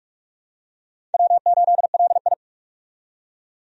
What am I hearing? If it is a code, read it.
Morse code, heard as a keyed tone, about 42 wpm, 700 Hz. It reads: W8LI
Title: USA Call Signs - 4 Characters - 45wpm